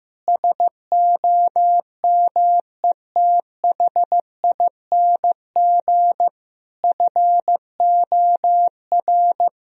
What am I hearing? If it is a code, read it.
Morse code, heard as a keyed tone, about 15 wpm, 700 Hz. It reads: SOMETHING FOR
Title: Sentences from Top 1000 Words - Review Part 1 - 15wpm